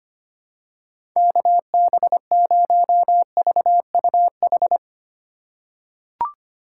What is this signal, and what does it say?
Morse code, heard as a keyed tone, about 25 wpm, 700 Hz. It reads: KB0VUH
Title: USA Call Signs - 8x Character Spacing - 25wpm